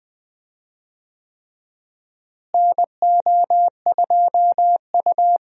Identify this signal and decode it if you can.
Morse code, heard as a keyed tone, about 20 wpm, 700 Hz. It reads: NO2U
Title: USA Call Signs - 4 Characters - Repeated 3x - 20wpm